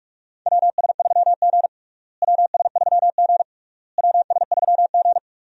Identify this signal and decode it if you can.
Morse code, heard as a keyed tone, about 46 wpm, 700 Hz. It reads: WS3G WS3G WS3G
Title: USA Call Signs - 4 Characters - Repeated 3x - 45wpm